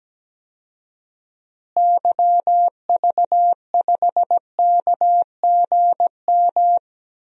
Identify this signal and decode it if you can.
Morse code, heard as a keyed tone, about 17 wpm, 700 Hz. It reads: YV5KGM
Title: DXCC-100 Call Signs - 6 Characters - Repeated 3x - 17wpm